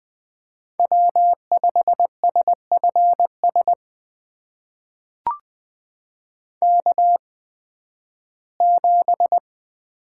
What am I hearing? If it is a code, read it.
Morse code, heard as a keyed tone, about 20 wpm, spaced as for 13 wpm, 700 Hz. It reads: W5SFS K7
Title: USA Call Signs - 8x Character Spacing - 20wpm